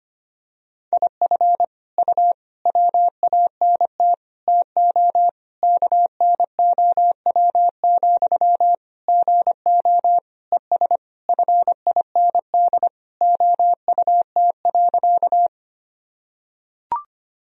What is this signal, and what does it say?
Morse code, heard as a keyed tone, about 25 wpm, 700 Hz. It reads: IF U WANT TO KNOW, GO ES FIND OUT.
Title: Sentences from Top 200 Words and Abbreviations 25wpm